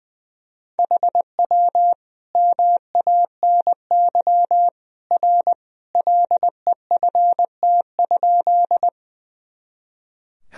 Morse code, heard as a keyed tone, about 20 wpm, 700 Hz. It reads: HW MANY R LEFT?
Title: Sentences from Top 300 Words and Abbreviations 20wpm